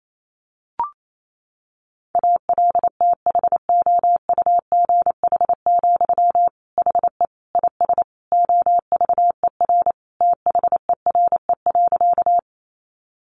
Morse code, heard as a keyed tone, about 28 wpm, 700 Hz. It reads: ALTHOUGH, HE IS OVER THERE.
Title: Sentences from Top 600 Words 28wpm